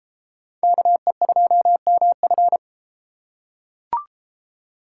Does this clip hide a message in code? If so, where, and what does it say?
Yes: Morse code from 0.6 to 2.6 s, heard as a keyed tone, about 31 wpm, 700 Hz. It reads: KE2MF